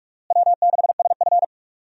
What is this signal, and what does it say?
Morse code, heard as a keyed tone, about 46 wpm, 700 Hz. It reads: W6SF
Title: USA Call Signs - 4 Characters - Repeated 3x - 45wpm